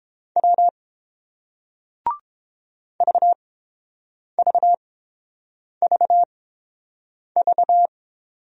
Morse code, heard as a keyed tone, about 29 wpm, spaced as for 7 wpm, 690 Hz. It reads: W VVVV